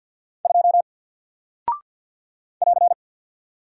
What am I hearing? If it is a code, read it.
Morse code, heard as a keyed tone, about 50 wpm, 700 Hz. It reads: 2 <AR>
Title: Single Character (Letter:Number:Punctuation:Prosign) 50wpm